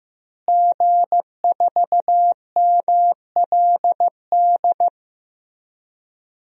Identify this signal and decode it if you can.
Morse code, heard as a keyed tone, about 15 wpm, 700 Hz. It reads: G4MLD
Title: DXCC-100 Call Signs - 5 Characters - Repeated 3x - 15wpm